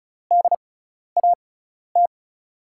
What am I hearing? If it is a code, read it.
Morse code, heard as a keyed tone, about 36 wpm, spaced as for 12 wpm, 700 Hz. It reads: DAT
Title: USA Call Signs - 5 Characters - 6x Character Spacing - 35wpm